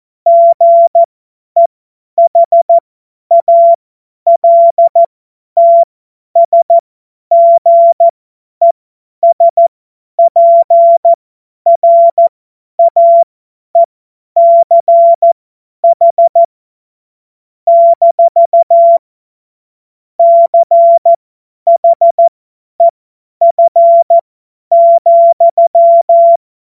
Morse code, heard as a keyed tone, about 14 wpm, spaced as for 10 wpm, 680 Hz. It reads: GEHALTSGESPRAECH - CHEF,
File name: Morsekurs / Witze / Witz 13. Dezember 2022 50 F70